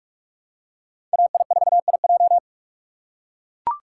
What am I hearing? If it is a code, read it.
Morse code, heard as a keyed tone, about 45 wpm, 700 Hz. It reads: AI4IJ